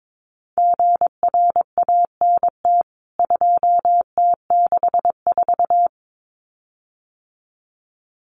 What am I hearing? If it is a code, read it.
Morse code, heard as a keyed tone, about 22 wpm, 700 Hz. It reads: GRANT 2T64
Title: CWT Exchange 22wpm